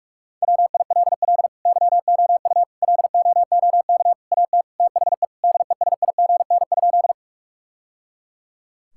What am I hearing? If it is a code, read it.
Morse code, heard as a keyed tone, about 46 wpm, 700 Hz. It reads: WILL YOU LOOK AT THE DESIGN?
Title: Sentences from Top 1000 Words - Review Part 5 - 45wpm